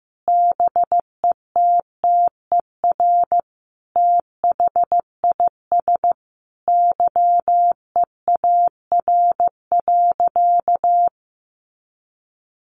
Morse code, heard as a keyed tone, about 15 wpm, 700 Hz. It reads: BETTER THIS YEAR.
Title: Sentences from Top 1500 Words 15wpm